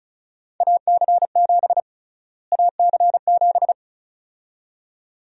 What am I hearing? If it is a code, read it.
Morse code, heard as a keyed tone, about 36 wpm, 700 Hz. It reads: AC7 AC7
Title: USA Call Sign Prefixes - Repeated 2x - 35wpm